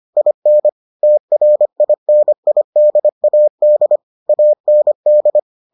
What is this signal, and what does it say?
Morse code, heard as a keyed tone, about 25 wpm, 600 Hz. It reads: IN TRINIDAD AND